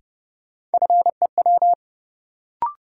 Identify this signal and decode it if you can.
Morse code, heard as a keyed tone, about 32 wpm, 700 Hz. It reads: FEW
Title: Sets of 2 Words - New Word Drill - 30wpm